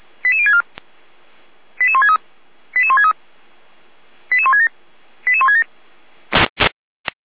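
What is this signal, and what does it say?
Five-tone selective call (ZVEI), heard as quick runs of five tones, tones of about 70 ms. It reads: 88064 88163 88163 88167 88167
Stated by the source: ZVEI five-tone selective call received on an SDR.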